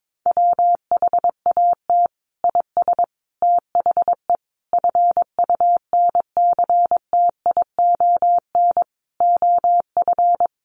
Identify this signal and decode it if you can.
Morse code, heard as a keyed tone, about 22 wpm, 700 Hz. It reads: WHAT IS THE FUNCTION OF
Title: Sentences from Top 1100 Words 22wpm